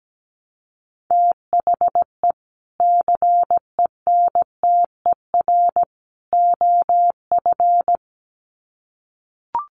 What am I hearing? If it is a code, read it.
Morse code, heard as a keyed tone, about 17 wpm, 700 Hz. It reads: THE CENTER OF
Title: Sets of 3 Words 17wpm